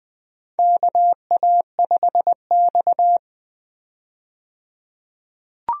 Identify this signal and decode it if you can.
Morse code, heard as a keyed tone, about 21 wpm, 700 Hz. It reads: KA5X
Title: USA Call Signs - 4 Characters - 20wpm